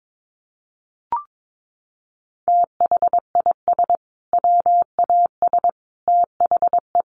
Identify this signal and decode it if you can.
Morse code, heard as a keyed tone, about 22 wpm, 700 Hz. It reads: THIS WAS THE
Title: Sets of 5 Words Encore 22wpm